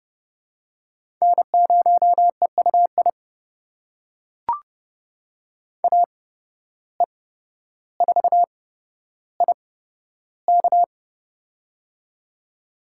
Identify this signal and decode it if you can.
Morse code, heard as a keyed tone, about 31 wpm, spaced as for 13 wpm, 700 Hz. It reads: N0EUI AE4IK
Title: USA Call Signs - 5 Characters - 8x Character Spacing - 30wpm